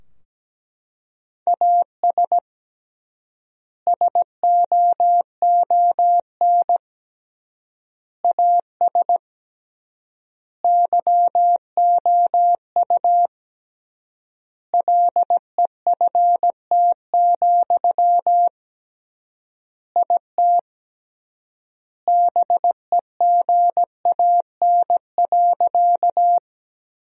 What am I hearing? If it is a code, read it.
Morse code, heard as a keyed tone, about 17 wpm, 700 Hz. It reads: AS SOON AS YOU LEFT, IT BEGAN.
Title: 3x Word-Spacing - Sentences from Top 400 Words - 17wpm